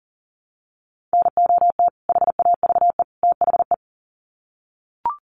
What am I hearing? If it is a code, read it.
Morse code, heard as a keyed tone, about 40 wpm, 700 Hz. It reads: NOT HAVE THE
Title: Sets of 3 Words 40wpm